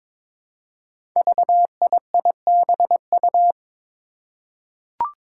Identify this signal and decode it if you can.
Morse code, heard as a keyed tone, about 22 wpm, 700 Hz. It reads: VIIBU
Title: Five Challenging Letter-Number Combinations 22wpm